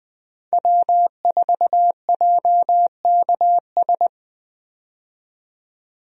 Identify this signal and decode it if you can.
Morse code, heard as a keyed tone, about 20 wpm, 700 Hz. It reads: W4JKS
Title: USA Call Signs - 5 Characters - 20wpm